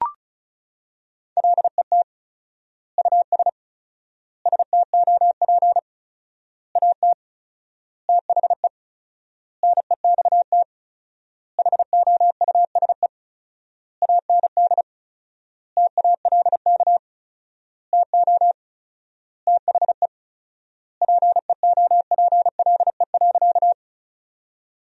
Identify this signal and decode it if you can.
Morse code, heard as a keyed tone, about 36 wpm, 700 Hz. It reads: LET US STOP AT THE NEXT HOUSE AND TALK TO THE PEOPLE.
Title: 4x Word-Spacing - Sentences from Top 400 Words - 35wpm